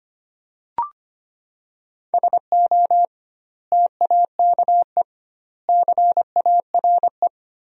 Morse code, heard as a keyed tone, about 25 wpm, 700 Hz. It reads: SO TAKE CARE
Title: 2x Word-Spacing - Sentences from Top 300 Words - 25wpm